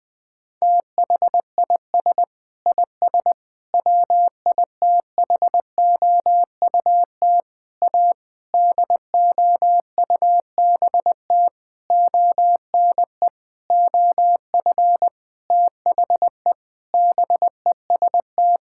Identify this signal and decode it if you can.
Morse code, heard as a keyed tone, about 20 wpm, 700 Hz. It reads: THIS IS WITHOUT A DOUBT ONE OF THE BEST
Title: Sentences with 7-Word Phrases - Spoken Beginning - 20wpm